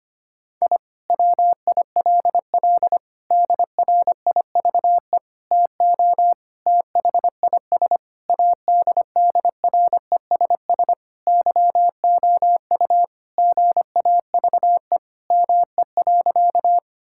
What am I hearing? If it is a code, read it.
Morse code, heard as a keyed tone, about 25 wpm, 700 Hz. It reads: I WILL DRIVE TO THIS ADDRESS YOU GAVE ME.
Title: Sentences from Top 1000 Words - Review Part 6 - 25wpm